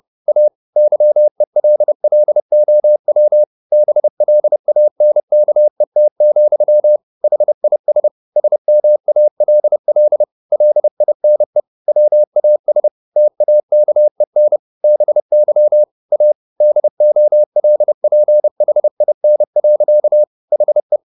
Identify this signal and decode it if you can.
Morse code, heard as a keyed tone, about 30 wpm, 600 Hz. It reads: A YELLOW BLANKET, HIS SMALL LINE WAS TAKEN BY A DOLPHIN. HE